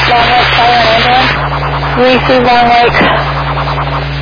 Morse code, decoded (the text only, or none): none